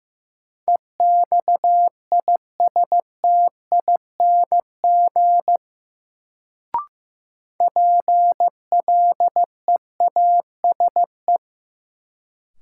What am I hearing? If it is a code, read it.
Morse code, heard as a keyed tone, about 15 wpm, 700 Hz. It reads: EXISTING PLEASE
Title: Sentences from Top 1500 Words - New Word Drill - 15wpm